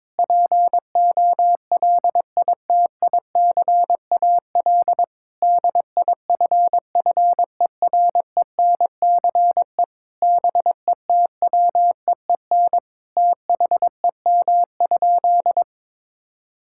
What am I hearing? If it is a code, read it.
Morse code, heard as a keyed tone, about 22 wpm, 700 Hz. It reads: POLITICAL DIFFERENCE BETWEEN THEM?
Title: Sentences from Top 1000 Words - Review Part 5 - 22wpm